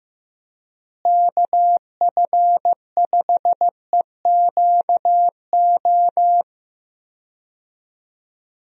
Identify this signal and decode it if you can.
Morse code, heard as a keyed tone, about 15 wpm, 700 Hz. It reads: KF5EQO
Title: USA Call Signs - 6 Characters - Repeated 2x - 15wpm